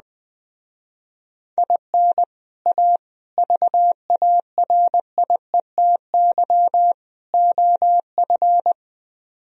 Morse code, heard as a keyed tone, about 20 wpm, 700 Hz. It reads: IN A VARIETY OF
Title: Speed-Racing - Sets of 4 Words 17wpm